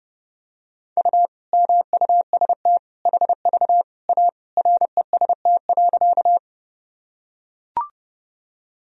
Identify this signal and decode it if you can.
Morse code, heard as a keyed tone, about 32 wpm, 700 Hz. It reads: U MUST HV A REST.
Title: Sentences from Top 500 Words and Abbreviations 30wpm